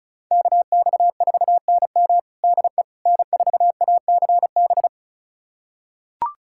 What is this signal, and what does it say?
Morse code, heard as a keyed tone, about 36 wpm, 700 Hz. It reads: KX4NM DE N4ACB
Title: USA Callsign DE Callsign 35wpm